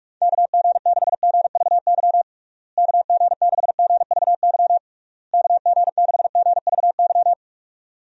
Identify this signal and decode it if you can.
Morse code, heard as a keyed tone, about 46 wpm, 700 Hz. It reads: KG6GVY KG6GVY KG6GVY
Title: USA Call Signs - 6 Characters - Repeated 3x - 45wpm